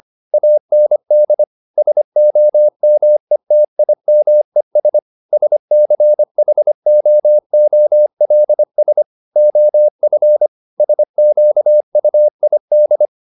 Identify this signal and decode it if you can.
Morse code, heard as a keyed tone, about 25 wpm, 600 Hz. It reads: AND SOMETIMES SCHOOLS OF SQUID
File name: the old man and the sea part 1 25wpm